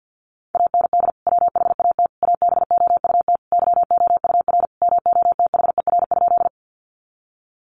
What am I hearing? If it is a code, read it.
Morse code, heard as a keyed tone, about 50 wpm, 700 Hz. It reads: AND WHAT ABOUT YOUR MOTHER?
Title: Sentences from Top 200 Words 50wpm